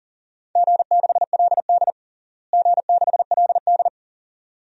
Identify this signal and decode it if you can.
Morse code, heard as a keyed tone, about 40 wpm, 700 Hz. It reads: G6LD G6LD
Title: DXCC-100 Call Signs - 4 Characters - Repeated 2x - 40wpm